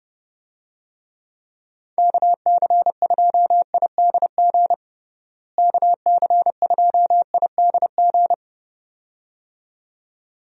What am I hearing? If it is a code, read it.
Morse code, heard as a keyed tone, about 32 wpm, 700 Hz. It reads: KC2IDG KC2IDG
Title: USA Call Signs - 6 Characters - Repeated 2x - 30wpm